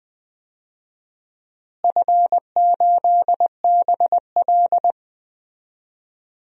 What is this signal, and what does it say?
Morse code, heard as a keyed tone, about 21 wpm, 700 Hz. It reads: F8BL
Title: DXCC-100 Call Signs - 4 Characters - Repeated 2x - 20wpm